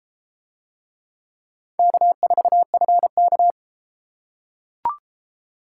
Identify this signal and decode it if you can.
Morse code, heard as a keyed tone, about 33 wpm, 700 Hz. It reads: K4FK